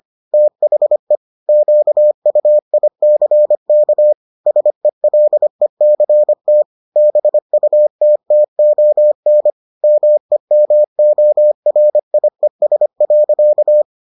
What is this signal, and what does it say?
Morse code, heard as a keyed tone, about 25 wpm, 600 Hz. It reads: THE QUICK SELECT BUTTON MEMORIES.